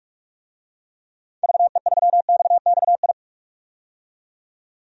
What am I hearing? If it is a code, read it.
Morse code, heard as a keyed tone, about 41 wpm, 700 Hz. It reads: VE3XXI
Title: DXCC-100 Call Signs - 6 Characters - 45wpm